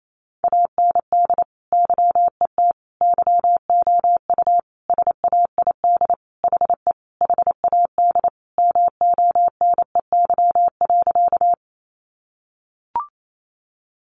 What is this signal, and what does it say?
Morse code, heard as a keyed tone, about 28 wpm, 700 Hz. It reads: AND YET YOU SAID HE HAD MONEY.
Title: Sentences from Top 500 Words 28wpm